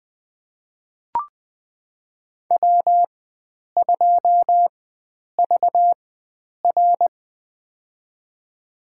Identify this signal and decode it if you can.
Morse code, heard as a keyed tone, about 21 wpm, spaced as for 9 wpm, 700 Hz. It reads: W2VR
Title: USA Call Signs - 4 Characters - 4x Character Spacing - 20wpm